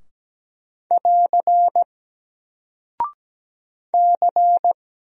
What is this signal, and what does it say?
Morse code, heard as a keyed tone, about 17 wpm, 700 Hz. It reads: <AR> C
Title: Single Character (Letter:Number:Punctuation:Prosign) 17wpm